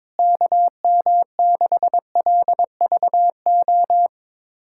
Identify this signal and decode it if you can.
Morse code, heard as a keyed tone, about 22 wpm, 700 Hz. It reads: KM6LVO
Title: USA Callsign DE Callsign 22wpm